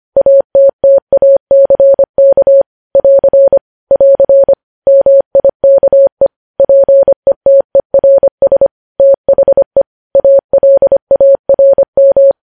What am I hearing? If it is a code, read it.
Morse code, heard as a keyed tone, about 25 wpm, 550 Hz. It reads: ATTACK <AR> <AR> MIKE PETERS THE ALARM